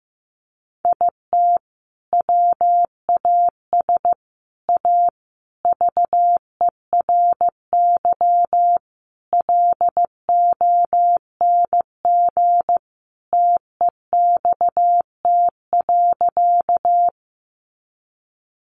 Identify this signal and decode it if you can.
Morse code, heard as a keyed tone, about 14 wpm, 700 Hz. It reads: IT WAS A VERY LONG TEXT.
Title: Sentences from Top 1200 Words 15wpm